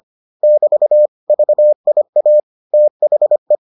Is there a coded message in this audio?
Yes: Morse code from 0.4 to 3.6 s, heard as a keyed tone, about 25 wpm, 600 Hz. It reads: <BT> VIA THE